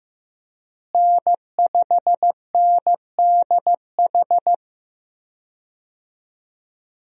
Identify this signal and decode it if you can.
Morse code, heard as a keyed tone, about 15 wpm, 700 Hz. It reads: N5NDH
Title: USA Call Signs - 5 Characters - Repeated 2x - 15wpm